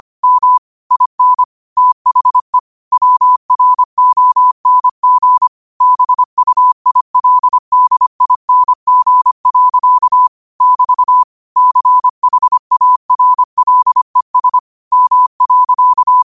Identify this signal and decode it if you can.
Morse code, heard as a keyed tone, about 25 wpm, 1000 Hz. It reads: M IN THE WRONG BUILDING. <BT> CHARLES M.
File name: qotdfiles / 25wpm